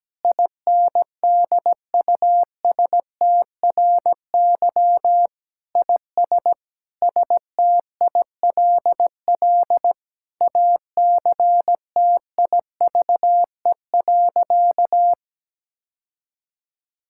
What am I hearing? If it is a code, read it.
Morse code, heard as a keyed tone, about 17 wpm, 700 Hz. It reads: INDUSTRY IS STILL ACTIVE.
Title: Sentences from Top 1000 Words - Review Part 6 - 17wpm